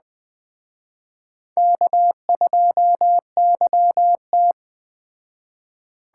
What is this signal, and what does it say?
Morse code, heard as a keyed tone, about 20 wpm, 700 Hz. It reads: K2YT